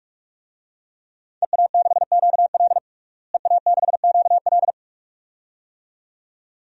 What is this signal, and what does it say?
Morse code, heard as a keyed tone, about 46 wpm, 700 Hz. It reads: EA6QL EA6QL
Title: DXCC-100 Call Signs - 5 Characters - Repeated 2x - 45wpm